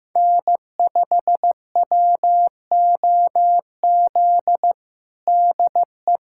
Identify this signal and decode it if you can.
Morse code, heard as a keyed tone, about 15 wpm, 700 Hz. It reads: N5WOZ DE
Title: USA Callsign DE Callsign - 5 Characters - 15wpm